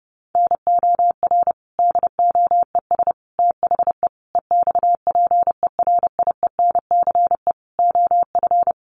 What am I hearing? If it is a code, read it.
Morse code, heard as a keyed tone, about 30 wpm, 700 Hz. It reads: NOR DOES THE EXPERIENCE OF
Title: Sentences from Top 1500 Words 30wpm